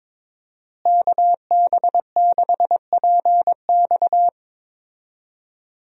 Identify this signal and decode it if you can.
Morse code, heard as a keyed tone, about 21 wpm, 700 Hz. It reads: KB6PX
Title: USA Call Signs - 5 Characters - Repeated 3x - 22wpm